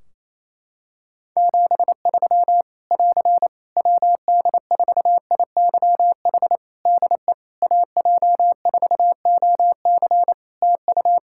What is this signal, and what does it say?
Morse code, heard as a keyed tone, about 28 wpm, 700 Hz. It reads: 73 <AR> WD4IYH DE AJ4OC TU
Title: Practice QSO - Exchange 4 - Ending - 28wpm